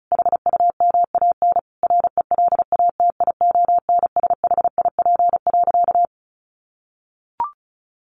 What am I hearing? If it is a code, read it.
Morse code, heard as a keyed tone, about 34 wpm, 700 Hz. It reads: HUMAN RELATIONSHIP.
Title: Sentences from Top 1100 Words 35wpm